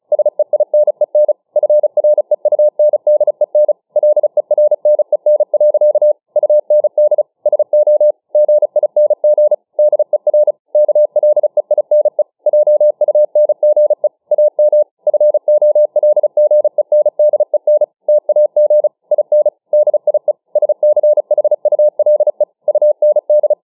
Morse code, heard as a keyed tone, about 34 wpm, 600 Hz. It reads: SEINEN FREUNDEN LERNEN. UND SO GING DER KLEINE JUNGE AM FOLGENDEN TAG IN DIE SCHULE UND
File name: dk5ke / Morsemogeln 35WpM